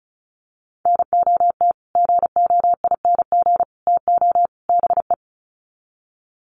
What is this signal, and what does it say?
Morse code, heard as a keyed tone, about 34 wpm, 700 Hz. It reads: NOT GOING TO BE
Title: Sets of 4 Words 35wpm